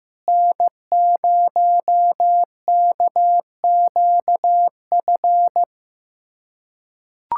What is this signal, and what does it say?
Morse code, heard as a keyed tone, about 15 wpm, 700 Hz. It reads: N0KQF
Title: USA Callsign DE Callsign - 5 Characters - 15wpm